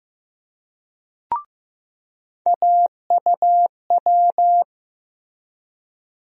Morse code, heard as a keyed tone, about 15 wpm, 700 Hz. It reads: AUW
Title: USA Call Sign Postfixes - 3 Letters 15wpm